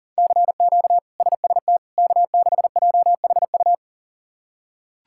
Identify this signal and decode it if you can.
Morse code, heard as a keyed tone, about 40 wpm, 700 Hz. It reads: CQ SST K6JHU
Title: K1USN SST QSO Practice - 40wpm